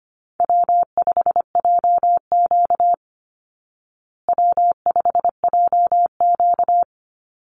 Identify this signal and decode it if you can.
Morse code, heard as a keyed tone, about 25 wpm, 700 Hz. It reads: W5JQ W5JQ
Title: Call Signs Repeated 2x - 25wpm